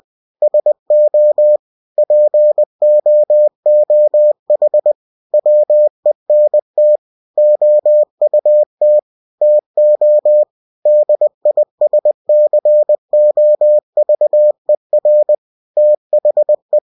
Morse code, heard as a keyed tone, about 20 wpm, 600 Hz. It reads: SO POOH WENT OUT TO DISCOVER THE
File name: cw-books / winnie-the-pooh / winnie-20wpm-page-0093